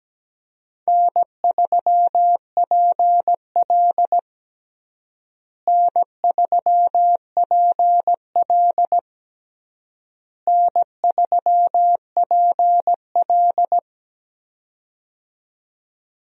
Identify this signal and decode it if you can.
Morse code, heard as a keyed tone, about 17 wpm, 700 Hz. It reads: N3PL N3PL N3PL
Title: USA Call Signs - 4 Characters - Repeated 3x - 17wpm